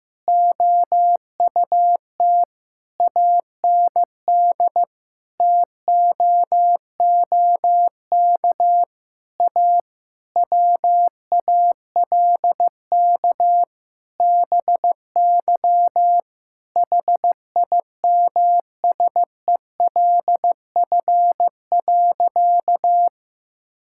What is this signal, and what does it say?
Morse code, heard as a keyed tone, about 15 wpm, 700 Hz. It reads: OUT AND TOOK A WALK BY HIMSELF.
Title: Sentences from Top 1600 Words 15wpm